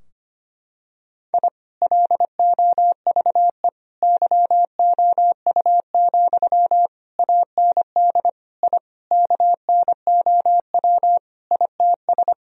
Morse code, heard as a keyed tone, about 25 wpm, 700 Hz. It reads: I LOVE YOU, AND I KNOW ITS
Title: Sentences from Top 1000 Words - Review Part 2 - 25wpm